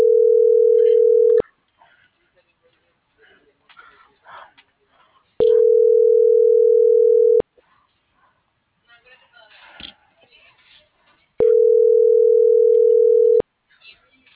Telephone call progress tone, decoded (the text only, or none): ringback tone